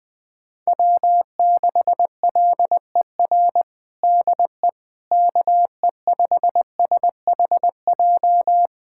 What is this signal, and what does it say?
Morse code, heard as a keyed tone, about 20 wpm, 700 Hz. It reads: W6LER DE KE5SHJ
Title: USA Callsign DE Callsign 20wpm